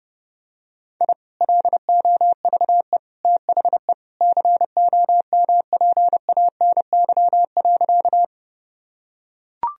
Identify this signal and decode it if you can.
Morse code, heard as a keyed tone, about 32 wpm, 700 Hz. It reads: I LOVE THE COMPANY.
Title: Sentences from Top 1000 Words - Review Part 3 - 30wpm